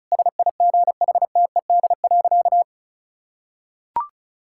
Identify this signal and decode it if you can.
Morse code, heard as a keyed tone, about 36 wpm, 700 Hz. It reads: SIGHTED.
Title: Pangram Challenge v2 - 35wpm